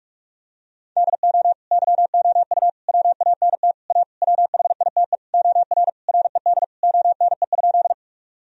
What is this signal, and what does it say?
Morse code, heard as a keyed tone, about 42 wpm, 700 Hz. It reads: DO YOU WANT A WHITE OR RED ONE?
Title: Sentences from Top 1000 Words - Review Part 2 - 45wpm